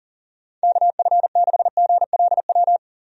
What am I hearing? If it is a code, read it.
Morse code, heard as a keyed tone, about 39 wpm, 700 Hz. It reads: KF6GLW
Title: USA Call Signs - 6 Characters - 40wpm